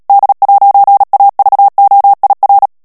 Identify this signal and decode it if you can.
Morse code, heard as a keyed tone, about 37 wpm, 800 Hz. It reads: D'AVOIR
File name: CW / FICHIERS-MP3 / FRANCAIS TEXTE CLAIR 2 VITESSE 35 MOTS MINUTES